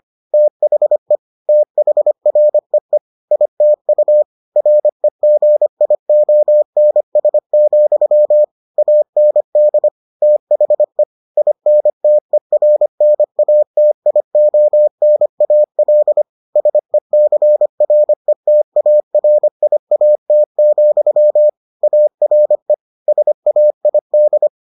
Morse code, heard as a keyed tone, about 25 wpm, 600 Hz. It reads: THE THREE ITU REGIONS, AND THE INTERNATIONAL SECRETARIAT, ARE SAID